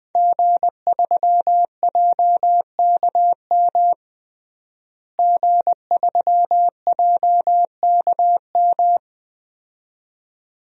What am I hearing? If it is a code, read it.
Morse code, heard as a keyed tone, about 21 wpm, 700 Hz. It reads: G3JKM G3JKM
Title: DXCC-100 Call Signs - 5 Characters - Repeated 2x - 20wpm